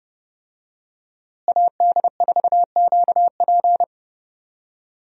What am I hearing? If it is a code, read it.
Morse code, heard as a keyed tone, about 30 wpm, 700 Hz. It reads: AD4QP
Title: USA Call Signs - 5 Characters - Repeated 3x - 30wpm